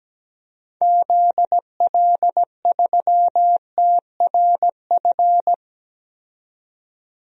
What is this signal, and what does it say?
Morse code, heard as a keyed tone, about 17 wpm, 700 Hz. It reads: ZL3TRF
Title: DXCC-100 Call Signs - 6 Characters - 17wpm